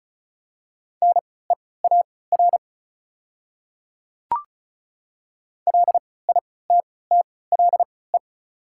Morse code, heard as a keyed tone, about 36 wpm, spaced as for 20 wpm, 700 Hz. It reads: NEAR LITTLE
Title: 3x Character Spacing - Top 300 Words - 35wpm